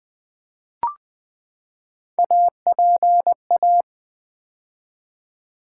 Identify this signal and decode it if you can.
Morse code, heard as a keyed tone, about 20 wpm, 700 Hz. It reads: APA
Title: Build A Word - 3 Letters At A Time - 9-Letter Words - 20wpm